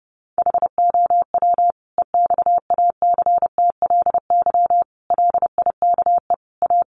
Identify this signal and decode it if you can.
Morse code, heard as a keyed tone, about 30 wpm, 700 Hz. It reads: HOW EXACTLY LIKE A
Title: Sentences from Top 900 Words 30wpm